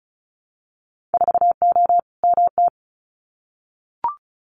Morse code, heard as a keyed tone, about 36 wpm, 700 Hz. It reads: 4O MT
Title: Winter Field Day Exchanges Encore 35wpm